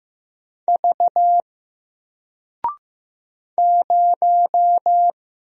Morse code, heard as a keyed tone, about 15 wpm, 700 Hz. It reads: V 0
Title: Single Character (Letter:Number:Punctuation:Prosign) Alt 15wpm